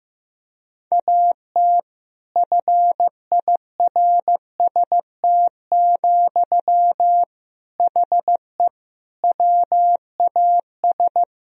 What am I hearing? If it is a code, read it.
Morse code, heard as a keyed tone, about 15 wpm, 700 Hz. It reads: AT FIRST, HE WAS
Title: Sentences from Top 1000 Words - Review Part 4 - 15wpm